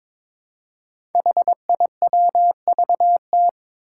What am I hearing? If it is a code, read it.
Morse code, heard as a keyed tone, about 23 wpm, 700 Hz. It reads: HIWVT
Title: Five Challenging Letter-Number Combinations 22wpm